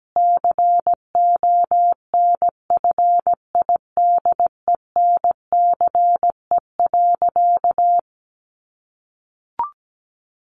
Morse code, heard as a keyed tone, about 17 wpm, 700 Hz. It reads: CONFIDENCE.
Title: Sentences from Top 1500 Words 17wpm